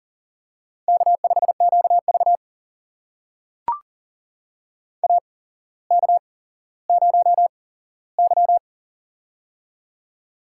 Morse code, heard as a keyed tone, about 39 wpm, spaced as for 17 wpm, 700 Hz. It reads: K5QV AK0Y
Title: USA Call Signs - 4 Characters - 8x Character Spacing - 40wpm